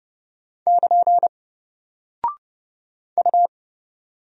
Morse code, heard as a keyed tone, about 30 wpm, 700 Hz. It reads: <KN> U
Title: Single Character (Letter:Number:Punctuation:Prosign) Alt 30wpm